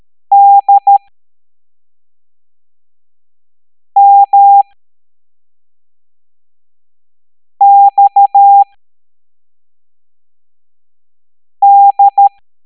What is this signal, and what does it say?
Morse code, heard as a keyed tone, about 13 wpm, spaced as for 3 wpm, 800 Hz. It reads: DMXD
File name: hamdata / L02-EX01-MDX